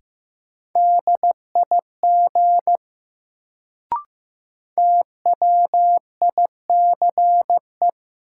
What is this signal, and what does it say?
Morse code, heard as a keyed tone, about 15 wpm, 700 Hz. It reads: DIG TWICE
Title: Sentences from Top 1600 Words - New Word Drill Alt - 15wpm